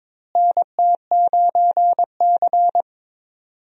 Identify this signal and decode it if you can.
Morse code, heard as a keyed tone, about 22 wpm, 700 Hz. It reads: NT9C
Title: USA Call Signs - 4 Characters - Repeated 2x - 22wpm